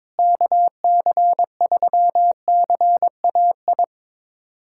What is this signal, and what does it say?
Morse code, heard as a keyed tone, about 22 wpm, 700 Hz. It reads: KC3CAI
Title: USA Callsign DE Callsign - 6 Characters - 22wpm